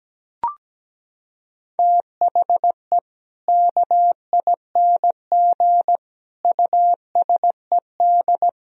Morse code, heard as a keyed tone, about 17 wpm, 700 Hz. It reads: THE KING USED
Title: Sentences from Top 1000 Words - Review Part 4 - 17wpm